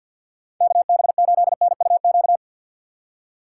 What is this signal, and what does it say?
Morse code, heard as a keyed tone, about 49 wpm, 700 Hz. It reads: KB8NUQ
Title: USA Call Signs - 6 Characters - 6x Character Spacing - 50wpm